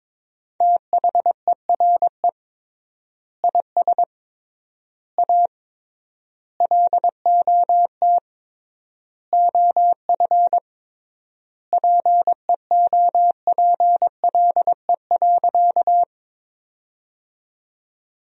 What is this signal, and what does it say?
Morse code, heard as a keyed tone, about 21 wpm, 700 Hz. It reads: THERE IS A LOT OF PEOPLE.
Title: 3x Word-Spacing - Sentences from Top 500 Words - 22wpm